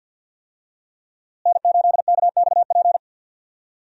Morse code, heard as a keyed tone, about 46 wpm, 700 Hz. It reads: N8KXP
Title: USA Call Signs - 5 Characters - Repeated 3x - 50wpm